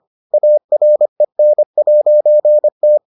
Morse code, heard as a keyed tone, about 25 wpm, 600 Hz. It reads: AREN'T